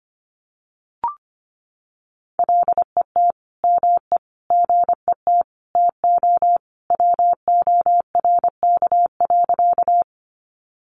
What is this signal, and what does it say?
Morse code, heard as a keyed tone, about 25 wpm, 700 Hz. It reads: LET ME GET TO WORK.
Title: Sentences from Top 300 Words 25wpm